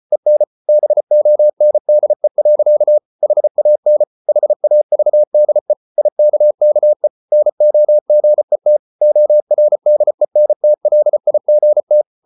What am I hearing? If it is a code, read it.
Morse code, heard as a keyed tone, about 34 wpm, 600 Hz. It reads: EN BONDE. HAN HAVDE IKKE NOGET ORDENTLIGT
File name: Ferie QRQ 34wpm